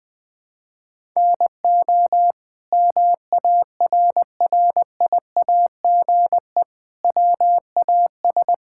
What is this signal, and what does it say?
Morse code, heard as a keyed tone, about 21 wpm, 700 Hz. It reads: NO MARRIAGE WAS
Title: Sentences from Top 1000 Words - Review Part 5 - 20wpm